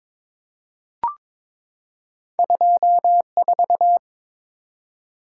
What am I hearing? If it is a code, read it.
Morse code, heard as a keyed tone, about 23 wpm, 700 Hz. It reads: 24
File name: practice / 22wpm / Two Letter-Number Combinations - 22wpm